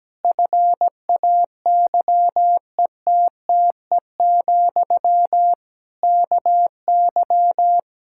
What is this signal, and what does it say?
Morse code, heard as a keyed tone, about 17 wpm, 700 Hz. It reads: FAYETTE, KY
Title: Practice QSO - Exchange 1 - Beginning - 17wpm